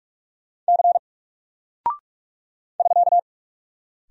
Morse code, heard as a keyed tone, about 46 wpm, 700 Hz. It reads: C <SK>
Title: Single Character (Letter:Number:Punctuation:Prosign) Alt 45wpm